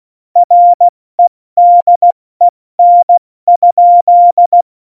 Morse code, heard as a keyed tone, about 16 wpm, 700 Hz. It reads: REDEN?
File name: Morsekurs / Witze / Witz 06. Oktober 2020, 70 F80